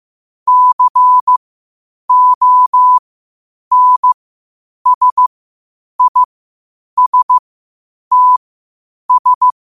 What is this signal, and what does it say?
Morse code, heard as a keyed tone, about 15 wpm, spaced as for 8 wpm, 1000 Hz. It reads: CONSISTS
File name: qotdfiles / 8wpm